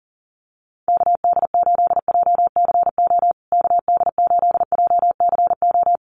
Morse code, heard as a keyed tone, about 40 wpm, 700 Hz. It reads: KD8JCO KD8JCO
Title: USA Call Signs Repeated 3x 40wpm